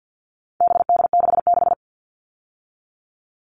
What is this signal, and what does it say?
Morse code, heard as a keyed tone, about 50 wpm, 700 Hz. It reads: BD66
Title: All Cut Numbers - 4-Digit 50wpm